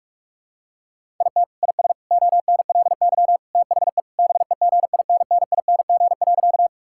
Morse code, heard as a keyed tone, about 41 wpm, 700 Hz. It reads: IT IS ONLY THE BEGINNING.
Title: Sentences from Top 1000 Words - Review Part 5 - 45wpm